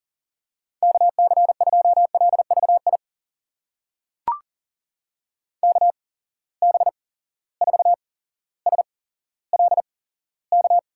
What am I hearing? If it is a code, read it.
Morse code, heard as a keyed tone, about 40 wpm, spaced as for 17 wpm, 700 Hz. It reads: KC2LVI KB4SLK
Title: USA Call Signs - 6 Characters - 8x Character Spacing - 40wpm